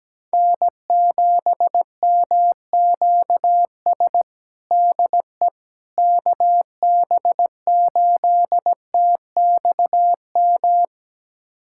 Morse code, heard as a keyed tone, about 17 wpm, 700 Hz. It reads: N7MQS DE KB8TXM
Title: USA Callsign DE Callsign 17wpm